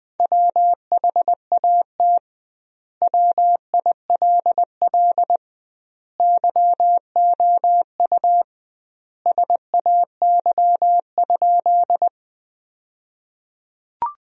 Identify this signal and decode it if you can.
Morse code, heard as a keyed tone, about 20 wpm, 700 Hz. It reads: WHAT WILL YOU SAY?
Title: 2x Word-Spacing - Sentences from Top 200 Words - 20wpm